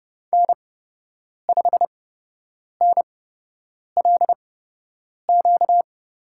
Morse code, heard as a keyed tone, about 30 wpm, spaced as for 8 wpm, 700 Hz. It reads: N5NLQ
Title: USA Call Signs - 8x Character Spacing - 30wpm